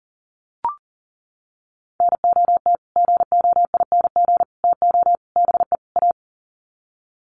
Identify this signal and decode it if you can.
Morse code, heard as a keyed tone, about 40 wpm, 700 Hz. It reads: NOT GOING TO BE A
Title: Sets of 5 Words Encore 40wpm